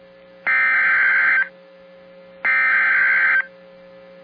Emergency Alert System (SAME) tones (header bursts), heard as harsh buzzing bursts. It is heard from 0.5 to 3.4 s.